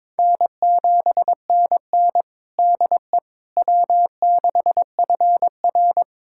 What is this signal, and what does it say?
Morse code, heard as a keyed tone, about 22 wpm, 700 Hz. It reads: N7NN DE W6FR
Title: USA Callsign DE Callsign - 4 Characters - 22wpm